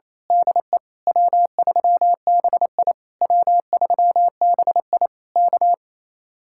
Morse code, heard as a keyed tone, about 26 wpm, 700 Hz. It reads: DE W3BI W3BI K
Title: Speed-Racing Common QSO Phrases - 28wpm